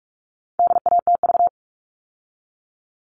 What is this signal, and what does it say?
Morse code, heard as a keyed tone, about 44 wpm, 700 Hz. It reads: DATV
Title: All Cut Numbers - 4-Digit 45wpm